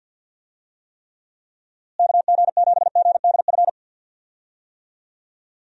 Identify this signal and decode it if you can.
Morse code, heard as a keyed tone, about 50 wpm, 700 Hz. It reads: KG7GDF
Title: USA Call Signs - 6 Characters - Repeated 3x - 50wpm